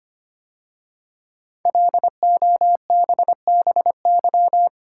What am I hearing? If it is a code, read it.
Morse code, heard as a keyed tone, about 25 wpm, 700 Hz. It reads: LOBBY
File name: 5 Letter Words - First 3 Letters Given - 25wpm